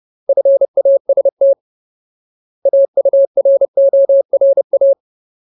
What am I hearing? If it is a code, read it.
Morse code, heard as a keyed tone, about 30 wpm, 550 Hz. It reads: FAST AURORA
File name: Ultimate Word List / 30WPM Ultimate